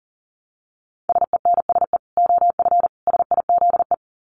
Morse code, heard as a keyed tone, about 40 wpm, 700 Hz. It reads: SENSE OF SIZE
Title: Dit Challenge 40wpm